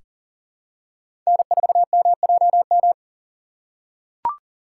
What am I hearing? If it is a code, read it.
Morse code, heard as a keyed tone, about 40 wpm, 700 Hz. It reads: N4MJM
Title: USA Call Signs - 5 Characters - 3x Character Spacing - 40wpm